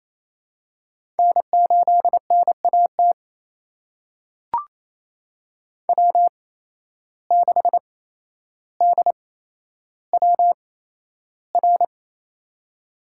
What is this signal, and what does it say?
Morse code, heard as a keyed tone, about 28 wpm, spaced as for 12 wpm, 700 Hz. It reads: N8NAT W6DWR
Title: USA Call Signs - 5 Characters - 8x Character Spacing - 28wpm